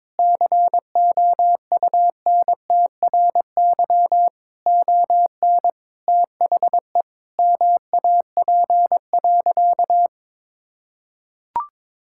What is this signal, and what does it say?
Morse code, heard as a keyed tone, about 21 wpm, 700 Hz. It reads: COUNTRY ON THE MAP.
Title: Sentences from Top 1000 Words - Review Part 6 - 22wpm